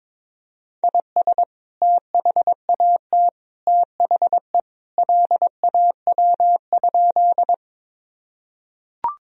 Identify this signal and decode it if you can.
Morse code, heard as a keyed tone, about 22 wpm, 700 Hz. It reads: IS THAT THE LAW?
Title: Sentences from Top 1000 Words - Review Part 3 - 22wpm